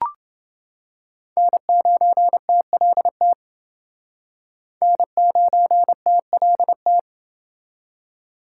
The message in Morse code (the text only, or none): N9TLT N9TLT